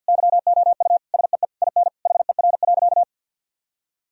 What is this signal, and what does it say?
Morse code, heard as a keyed tone, about 50 wpm, 700 Hz. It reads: YOU SEE IN HER.
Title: Sentences from Top 1000 Words - Review Part 4 - 50wpm